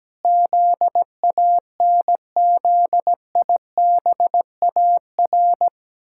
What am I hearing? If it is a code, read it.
Morse code, heard as a keyed tone, about 17 wpm, 700 Hz. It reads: ZANZIBAR
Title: Pangram Challenge v2 - 17wpm